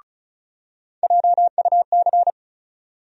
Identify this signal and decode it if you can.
Morse code, heard as a keyed tone, about 36 wpm, 700 Hz. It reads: JUC
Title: USA Call Sign Postfixes - 3 Letters 35wpm